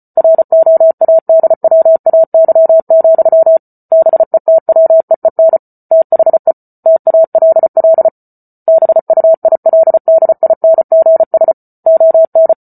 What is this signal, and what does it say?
Morse code, heard as a keyed tone, about 34 wpm, 650 Hz. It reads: ROADWAY, BETWEEN THE TALL BUILDINGS ON